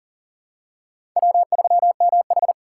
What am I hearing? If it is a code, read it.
Morse code, heard as a keyed tone, about 37 wpm, 700 Hz. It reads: W3MH
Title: USA Call Signs - 4 Characters - 4x Character Spacing - 40wpm